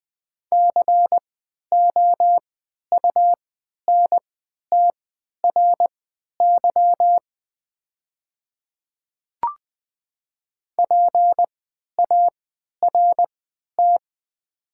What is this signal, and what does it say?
Morse code, heard as a keyed tone, about 21 wpm, spaced as for 11 wpm, 700 Hz. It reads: COUNTRY PART
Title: 3x Character Spacing - Top 300 Words - 20wpm